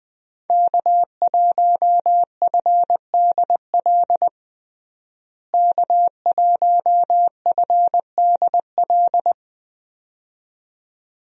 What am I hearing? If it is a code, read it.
Morse code, heard as a keyed tone, about 21 wpm, 700 Hz. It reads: K1FDL K1FDL
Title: USA Call Signs - 5 Characters - Repeated 2x - 20wpm